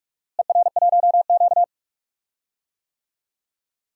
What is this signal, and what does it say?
Morse code, heard as a keyed tone, about 41 wpm, 700 Hz. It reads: ER1Q
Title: DXCC-100 Call Signs - 4 Characters - Repeated 2x - 45wpm